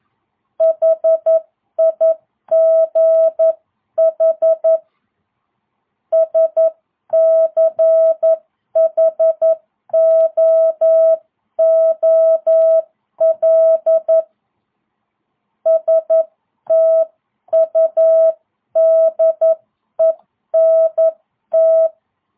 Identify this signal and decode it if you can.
Morse code, heard as a keyed tone, about 11 wpm, 660 Hz. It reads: HIGH SCHOOL STUDENT